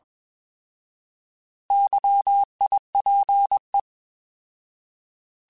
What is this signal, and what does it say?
Morse code, heard as a keyed tone, about 21 wpm, 800 Hz. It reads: YIPE
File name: Alphabetical / Alphabetical mp3s / Ys @20